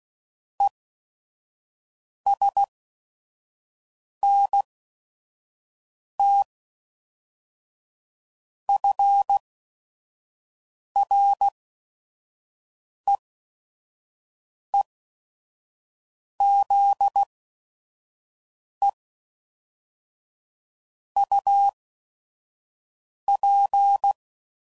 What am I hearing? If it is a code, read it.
Morse code, heard as a keyed tone, about 16 wpm, spaced as for 5 wpm, 800 Hz. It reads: ESNT FREEZE UP
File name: CW Code Groups @5WPM / QSOs / QSO WD0FBT